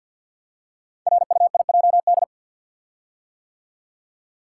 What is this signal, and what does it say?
Morse code, heard as a keyed tone, about 46 wpm, 700 Hz. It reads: RUIJD
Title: Code Groups - 5 Letters - 50wpm